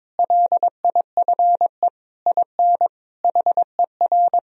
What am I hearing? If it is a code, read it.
Morse code, heard as a keyed tone, about 22 wpm, 700 Hz. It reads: LIFE IN HER
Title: Sentences from Top 1000 Words - Review Part 2 - 22wpm